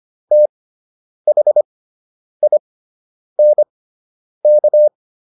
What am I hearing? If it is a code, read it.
Morse code, heard as a keyed tone, about 25 wpm, spaced as for 9 wpm, 600 Hz. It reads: THINK